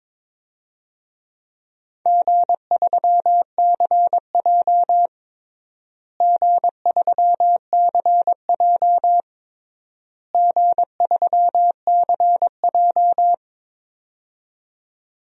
Morse code, heard as a keyed tone, about 22 wpm, 700 Hz. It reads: G3CJ G3CJ G3CJ
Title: DXCC-100 Call Signs - 4 Characters - Repeated 3x - 22wpm